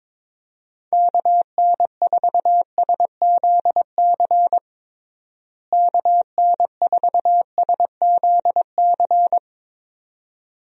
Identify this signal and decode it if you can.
Morse code, heard as a keyed tone, about 22 wpm, 700 Hz. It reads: KN4SZC KN4SZC
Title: USA Call Signs - 6 Characters - Repeated 2x - 22wpm